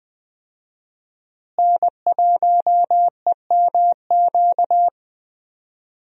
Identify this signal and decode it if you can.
Morse code, heard as a keyed tone, about 20 wpm, 700 Hz. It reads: N1EMQ
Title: USA Call Signs - 5 Characters - Repeated 2x - 20wpm